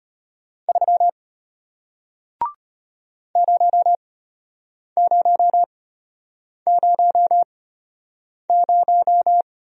Morse code, heard as a keyed tone, about 34 wpm, spaced as for 8 wpm, 700 Hz. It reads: 3 0000